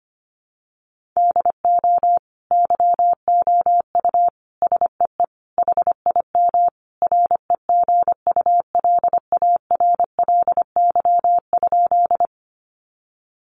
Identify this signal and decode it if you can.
Morse code, heard as a keyed tone, about 25 wpm, 700 Hz. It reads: DO YOU SEE HIM REGULARLY?
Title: Sentences from Top 1400 Words 25wpm